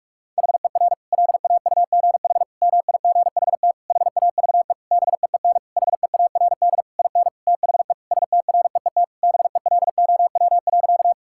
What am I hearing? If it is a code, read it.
Morse code, heard as a keyed tone, about 46 wpm, 700 Hz. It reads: HER LAUGH MIGHT HAVE BEEN HEARD IN THE STREET BELOW.
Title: Sentences from Top 1000 Words - Review Part 5 - 45wpm